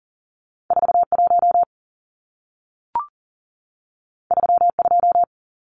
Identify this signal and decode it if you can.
Morse code, heard as a keyed tone, about 40 wpm, 700 Hz. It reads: 41 32
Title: Numbers - 2-Digit 40wpm